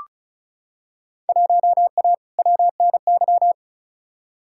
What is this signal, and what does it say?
Morse code, heard as a keyed tone, about 34 wpm, 700 Hz. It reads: 1A WNY
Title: Field Day Exchanges Encore - 35wpm